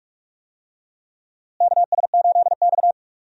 Morse code, heard as a keyed tone, about 47 wpm, 700 Hz. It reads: KS8X
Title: USA Call Signs - 4 Characters - Repeated 3x - 45wpm